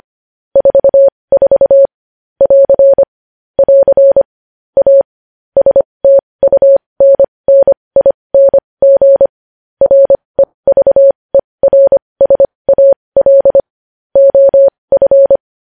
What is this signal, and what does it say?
Morse code, heard as a keyed tone, about 25 wpm, spaced as for 20 wpm, 550 Hz. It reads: 44 <AR> <AR> A STUNNING REVERSAL OF
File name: podcast / News headlines-1x-25x20